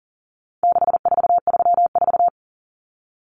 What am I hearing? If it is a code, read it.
Morse code, heard as a keyed tone, about 40 wpm, 700 Hz. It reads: 6434
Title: Numbers - 4-Digit 40wpm